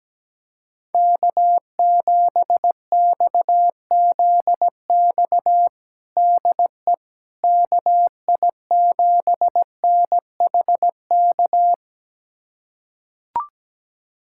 Morse code, heard as a keyed tone, about 17 wpm, 700 Hz. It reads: K7XZX DE KI7NHK
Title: USA Callsign DE Callsign 17wpm